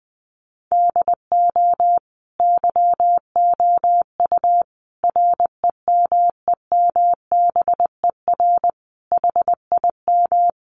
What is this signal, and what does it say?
Morse code, heard as a keyed tone, about 20 wpm, 700 Hz. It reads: DO YOU REMEMBER HIM
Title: Sentences from Top 1300 Words 20wpm